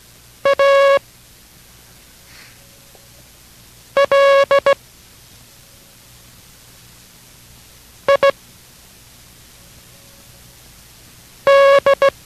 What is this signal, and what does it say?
Morse code, heard as a keyed tone, about 17 wpm, spaced as for 3 wpm, 560 Hz. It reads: ALID